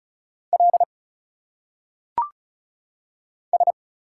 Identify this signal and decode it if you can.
Morse code, heard as a keyed tone, about 36 wpm, 700 Hz. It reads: L S
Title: Single Letters - Phonetic Pronunciation 35wpm